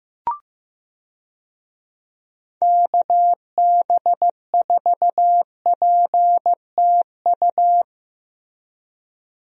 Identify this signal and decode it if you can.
Morse code, heard as a keyed tone, about 15 wpm, 700 Hz. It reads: KB4PTU
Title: USA Call Signs - 6 Characters - Repeated 2x - 15wpm